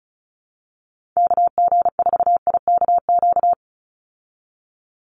Morse code, heard as a keyed tone, about 34 wpm, 700 Hz. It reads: KG4IKQ
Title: Call Signs Repeated 2x - 35wpm